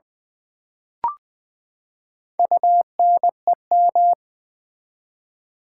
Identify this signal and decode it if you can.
Morse code, heard as a keyed tone, about 20 wpm, 700 Hz. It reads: UNEM
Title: Build A Word - 4 Letters At A Time - 12-Letter Words - 20wpm